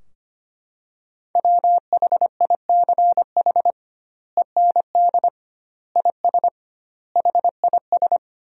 Morse code, heard as a keyed tone, about 25 wpm, 700 Hz. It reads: WHICH END IS HIS
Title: 2x Word-Spacing - Sentences from Top 300 Words - 25wpm